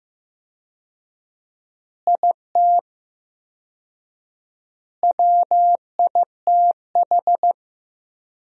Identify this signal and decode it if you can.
Morse code, heard as a keyed tone, about 15 wpm, 700 Hz. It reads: IT WITH
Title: 4x Word-Spacing - Sentences from Top 500 Words - 15wpm